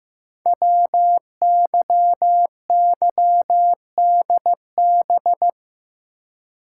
Morse code, heard as a keyed tone, about 15 wpm, 700 Hz. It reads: WYYDB
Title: Code Groups - 5 Letters - 15wpm